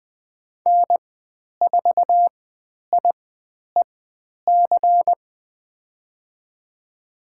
Morse code, heard as a keyed tone, about 20 wpm, spaced as for 10 wpm, 700 Hz. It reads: N4IEC
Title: 102 Lesson 17g - Letter u - Call Signs 20wpm